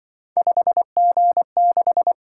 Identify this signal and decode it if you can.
Morse code, heard as a keyed tone, about 24 wpm, 700 Hz. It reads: 5G6
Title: Speed-Racing - Three Letter-Number Combinations - 20wpm